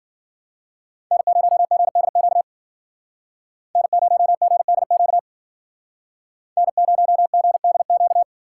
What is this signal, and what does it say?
Morse code, heard as a keyed tone, about 51 wpm, 700 Hz. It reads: N0GDQ N0GDQ N0GDQ